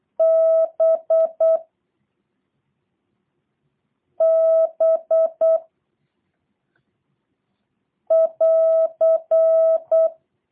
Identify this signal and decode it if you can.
Morse code, heard as a keyed tone, about 8 wpm, 650 Hz. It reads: B B <AR>
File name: jh8cbh / 2023cw / review1-6